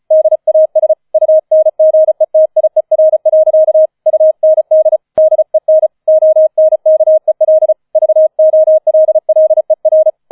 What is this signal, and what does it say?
Morse code, heard as a keyed tone, about 34 wpm, 630 Hz. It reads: DAS UNGETIER. UND DEN ONKEL VOLLER
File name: 005 Fünfter Streich 35wpm